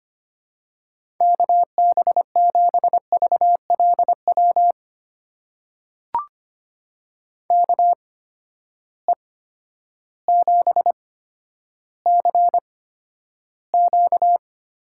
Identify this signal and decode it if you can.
Morse code, heard as a keyed tone, about 25 wpm, spaced as for 11 wpm, 700 Hz. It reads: KB7VLW KE7CQ
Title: USA Call Signs - 6 Characters - 8x Character Spacing - 25wpm